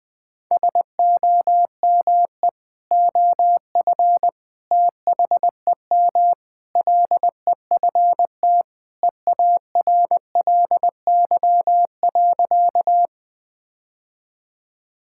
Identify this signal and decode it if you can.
Morse code, heard as a keyed tone, about 20 wpm, 700 Hz. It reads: SOME OF THEM LEFT EARLY.
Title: Sentences with 3-Word Phrases - Spoken Beginning - 20wpm